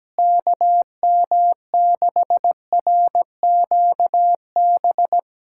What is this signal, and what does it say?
Morse code, heard as a keyed tone, about 17 wpm, 700 Hz. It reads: KM6RQB
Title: USA Callsign DE Callsign - 6 Characters - 17wpm